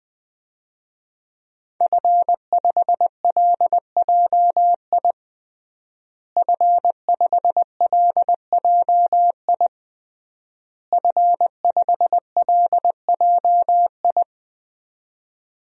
Morse code, heard as a keyed tone, about 21 wpm, 700 Hz. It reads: F5LJI F5LJI F5LJI
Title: DXCC-100 Call Signs - 5 Characters - Repeated 3x - 20wpm